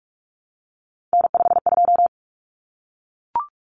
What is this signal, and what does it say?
Morse code, heard as a keyed tone, about 44 wpm, 700 Hz. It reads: N52
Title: Common Cut Numbers - 3-Digit 45wpm